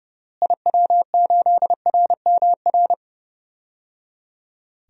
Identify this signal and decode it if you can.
Morse code, heard as a keyed tone, about 30 wpm, 700 Hz. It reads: IW8RMR
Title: DXCC-100 Call Signs - 6 Characters - Repeated 3x - 30wpm